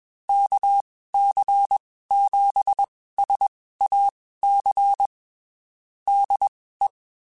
Morse code, heard as a keyed tone, about 21 wpm, spaced as for 15 wpm, 800 Hz. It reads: KC7SAC DE
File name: (spaced) / QSOs Spaced-mp3s / kc7sac @20 sp